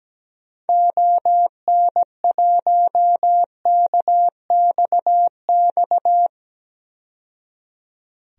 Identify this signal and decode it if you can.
Morse code, heard as a keyed tone, about 17 wpm, 700 Hz. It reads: ON1KXX
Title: DXCC-100 Call Signs - 6 Characters - 17wpm